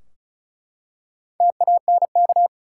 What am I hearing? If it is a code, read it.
Morse code, heard as a keyed tone, about 34 wpm, 700 Hz. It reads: TANK
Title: Four Letter Words - 35wpm